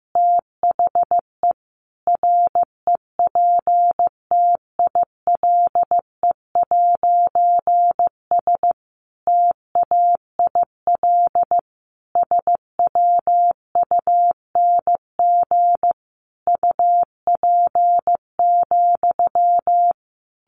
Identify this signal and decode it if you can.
Morse code, heard as a keyed tone, about 15 wpm, 700 Hz. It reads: THE REPTILE'S TAIL SWUNG UP,
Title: A Sound Of Thunder 15wpm